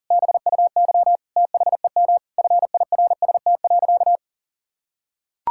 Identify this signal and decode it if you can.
Morse code, heard as a keyed tone, about 40 wpm, 700 Hz. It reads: BUY THEM FIRST.
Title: Sentences from Top 1000 Words - Review Part 6 - 40wpm